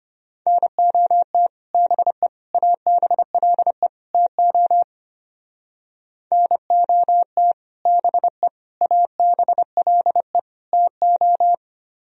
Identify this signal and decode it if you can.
Morse code, heard as a keyed tone, about 31 wpm, 700 Hz. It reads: NOT BE ABLE TO NOT BE ABLE TO
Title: Speed-Racing - Sets of 4 Words 25wpm